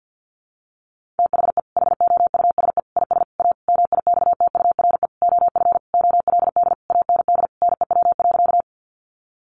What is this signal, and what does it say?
Morse code, heard as a keyed tone, about 50 wpm, 700 Hz. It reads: THE HOUSE IS A MIXTURE OF OLD AND NEW.
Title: Sentences from Top 1500 Words 50wpm